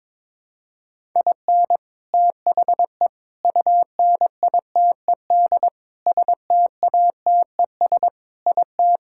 Morse code, heard as a keyed tone, about 22 wpm, 700 Hz. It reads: IN THE UNITED STATES IT
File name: practice / 22wpm / Sentences with 4-Word Phrases - 22wpm